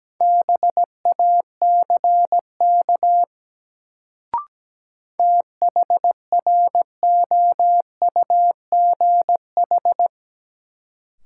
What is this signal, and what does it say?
Morse code, heard as a keyed tone, about 17 wpm, 700 Hz. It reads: BACK THROUGH
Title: Sets of 2 Words - New Word Drill - 17wpm